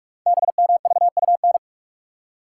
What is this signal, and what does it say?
Morse code, heard as a keyed tone, about 44 wpm, 700 Hz. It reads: BMVUN
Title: Code Groups - 5 Letters - 45wpm